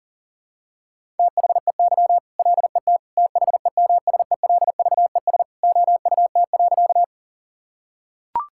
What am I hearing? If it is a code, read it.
Morse code, heard as a keyed tone, about 37 wpm, 700 Hz. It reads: THEY LET THEMSELVES OUT.
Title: Sentences from Top 1000 Words - Review Part 5 - 40wpm